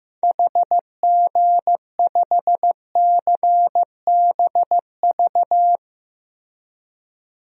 Morse code, heard as a keyed tone, about 16 wpm, 700 Hz. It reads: HG5CBV
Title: DXCC-100 Call Signs - 6 Characters - Repeated 2x - 15wpm